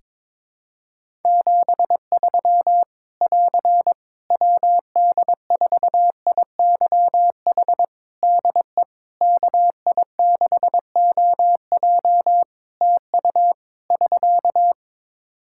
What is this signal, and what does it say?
Morse code, heard as a keyed tone, about 22 wpm, 700 Hz. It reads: 73 <AR> WD4IYH DE KI6OJ TU <SK>
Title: K1USN SST QSO Practice 22wpm